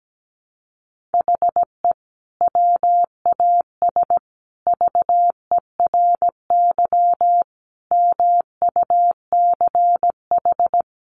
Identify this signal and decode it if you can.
Morse code, heard as a keyed tone, about 17 wpm, 700 Hz. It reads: HE WAS VERY MUCH
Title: Sentences from Top 1500 Words 17wpm